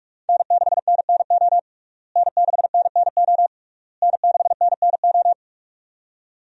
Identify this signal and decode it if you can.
Morse code, heard as a keyed tone, about 46 wpm, 700 Hz. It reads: N6NNO N6NNO N6NNO
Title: USA Call Signs - 5 Characters - Repeated 3x - 45wpm